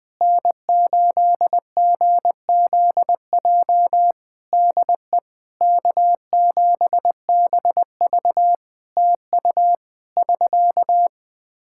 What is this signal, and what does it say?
Morse code, heard as a keyed tone, about 20 wpm, 700 Hz. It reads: N8GZJ DE K7BV TU <SK>
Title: Practice QSO - Exchange 4 - Ending - 20wpm